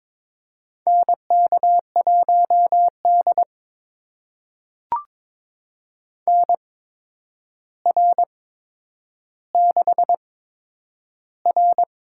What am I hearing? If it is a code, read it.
Morse code, heard as a keyed tone, about 22 wpm, spaced as for 9 wpm, 700 Hz. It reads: NK1D NR6R
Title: USA Call Signs - 8x Character Spacing - 22wpm